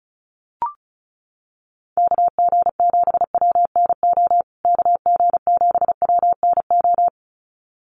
Morse code, heard as a keyed tone, about 34 wpm, 700 Hz. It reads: KG7WNO KG7WNO
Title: USA Call Signs Repeated 2x 35wpm